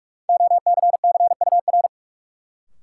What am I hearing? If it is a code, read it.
Morse code, heard as a keyed tone, about 46 wpm, 700 Hz. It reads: OCCUR
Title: Sentences from Top 900 Words - New Word Drill - 45wpm